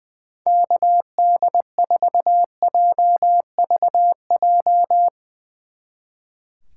Morse code, heard as a keyed tone, about 20 wpm, 700 Hz. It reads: KD4JVJ
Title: Speed Racing - USA 2x3 Call Signs 20wpm